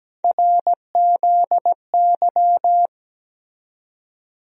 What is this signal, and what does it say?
Morse code, heard as a keyed tone, about 17 wpm, 700 Hz. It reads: RZY
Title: USA Call Sign Postfixes - 3 Letters 17wpm